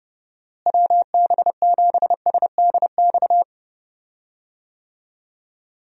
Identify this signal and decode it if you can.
Morse code, heard as a keyed tone, about 32 wpm, 700 Hz. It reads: WB7SDX
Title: USA Call Signs - 6 Characters - Repeated 3x - 30wpm